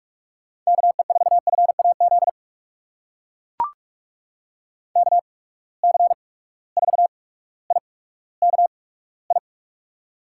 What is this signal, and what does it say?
Morse code, heard as a keyed tone, about 44 wpm, spaced as for 19 wpm, 700 Hz. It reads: KE4FAZ KC4IKI
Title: USA Call Signs - 8x Character Spacing - 45wpm